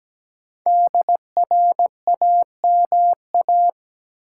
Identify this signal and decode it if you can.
Morse code, heard as a keyed tone, about 17 wpm, 700 Hz. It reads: DRAMA
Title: Sentences from Top 1500 Words - New Word Drill - 17wpm